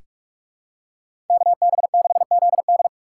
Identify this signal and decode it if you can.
Morse code, heard as a keyed tone, about 46 wpm, 700 Hz. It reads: KB6ZD
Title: USA Call Signs - 3x Character Spacing - 45wpm